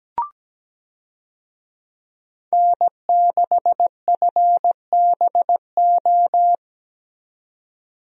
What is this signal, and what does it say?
Morse code, heard as a keyed tone, about 17 wpm, 700 Hz. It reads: N6FBO
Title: USA Call Signs - 5 Characters - 17wpm